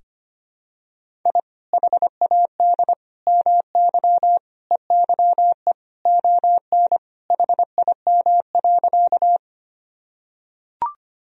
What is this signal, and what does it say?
Morse code, heard as a keyed tone, about 25 wpm, 700 Hz. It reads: I HAD MY EYE ON HIM.
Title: Sentences from Top 1000 Words - Review Part 1 - 25wpm